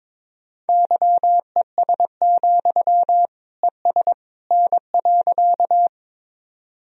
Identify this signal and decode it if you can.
Morse code, heard as a keyed tone, about 21 wpm, 700 Hz. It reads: YES, ES N.
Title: Sentences from Top 500 Words and Abbreviations 22wpm